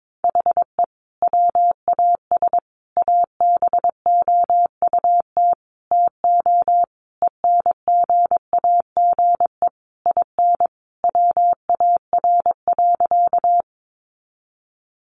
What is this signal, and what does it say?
Morse code, heard as a keyed tone, about 22 wpm, 700 Hz. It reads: HE WAS ABOUT TO ENGAGE IN WAR.
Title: Sentences from Top 1500 Words 22wpm